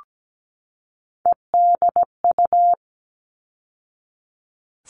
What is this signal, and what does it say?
Morse code, heard as a keyed tone, about 17 wpm, 700 Hz. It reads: EDU
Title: All Cut Numbers - 3-Digit 17wpm